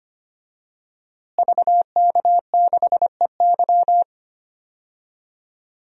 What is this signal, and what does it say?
Morse code, heard as a keyed tone, about 25 wpm, 700 Hz. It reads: VK6EY
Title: DXCC-100 Call Signs - 5 Characters - Repeated 2x - 25wpm